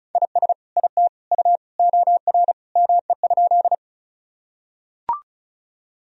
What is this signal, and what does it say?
Morse code, heard as a keyed tone, about 34 wpm, 700 Hz. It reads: IS IT U OR ME?
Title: Sentences from Top 200 Words and Abbreviations 35wpm